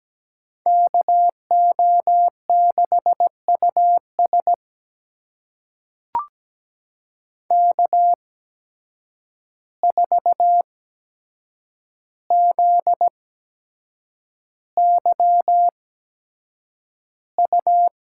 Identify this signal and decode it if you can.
Morse code, heard as a keyed tone, about 17 wpm, spaced as for 7 wpm, 700 Hz. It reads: KO6US K4ZYU
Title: USA Call Signs - 5 Characters - 8x Character Spacing - 17wpm